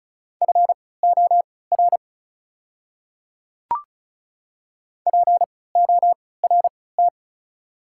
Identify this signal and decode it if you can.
Morse code, heard as a keyed tone, about 36 wpm, spaced as for 20 wpm, 690 Hz. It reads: FOR PORT
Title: 3x Character Spacing - Top 300 Words - 35wpm